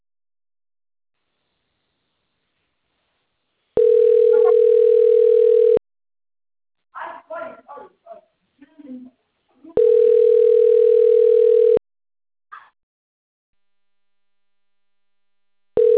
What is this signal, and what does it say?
Ringback tone (North America), 440 + 480 Hz, with speech.
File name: rg-1600-unknown- - - .53507